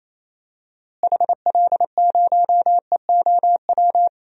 Morse code, heard as a keyed tone, about 27 wpm, 700 Hz. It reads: HL0EOW
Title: DXCC-100 Call Signs - 6 Characters - Repeated 2x - 28wpm